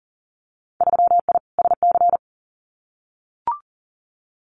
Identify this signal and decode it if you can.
Morse code, heard as a keyed tone, about 40 wpm, 700 Hz. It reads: 3I SC
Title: Winter Field Day Exchanges 40wpm